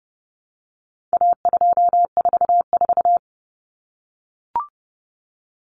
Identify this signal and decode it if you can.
Morse code, heard as a keyed tone, about 30 wpm, 700 Hz. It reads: A244
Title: Common Cut Numbers - 4-Digit 30wpm